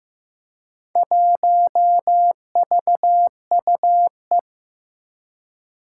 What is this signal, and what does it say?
Morse code, heard as a keyed tone, about 15 wpm, 700 Hz. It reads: 1VUE
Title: Four Challenging Letter-Number Combinations 15wpm